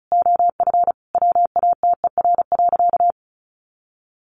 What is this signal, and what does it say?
Morse code, heard as a keyed tone, about 34 wpm, 700 Hz. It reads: OF WATER.
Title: Sentences from Top 1300 Words 35wpm